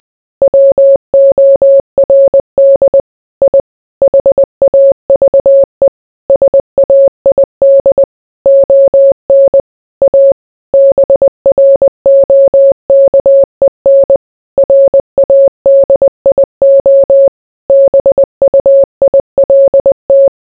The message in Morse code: WORD I HAVE SAID ON A BROKEN RADIO BUILT

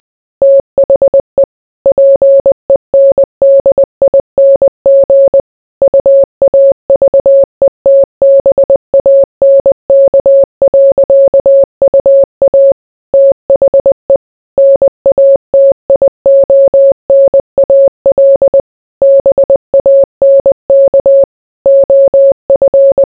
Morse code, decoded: THE PENDING UAVETBANK.UA THE NATIONAL BANK OF

